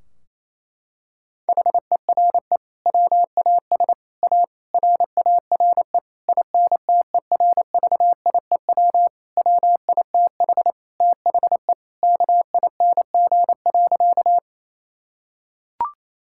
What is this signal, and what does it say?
Morse code, heard as a keyed tone, about 28 wpm, 700 Hz. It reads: HERE WAS A RARE INTERVIEW WITH THE KING.
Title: Sentences from Top 1600 Words 28wpm